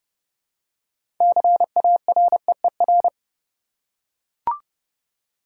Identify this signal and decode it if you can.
Morse code, heard as a keyed tone, about 30 wpm, 700 Hz. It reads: CAREER
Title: Six Letter Words - 30wpm